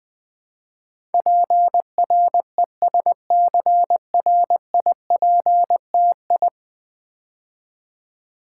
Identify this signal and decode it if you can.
Morse code, heard as a keyed tone, about 20 wpm, 700 Hz. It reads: PRESCRIPTI
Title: Build A Word - 5 Letters At A Time - 12-Letter Words - 20wpm